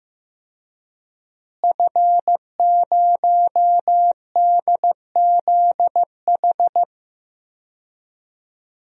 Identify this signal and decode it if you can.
Morse code, heard as a keyed tone, about 15 wpm, 700 Hz. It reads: F0DZH
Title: DXCC-100 Call Signs - 5 Characters - Repeated 2x - 15wpm